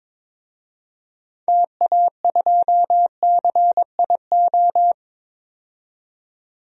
Morse code, heard as a keyed tone, about 22 wpm, 700 Hz. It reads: TA2CIO
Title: DXCC-100 Call Signs - 6 Characters - 22wpm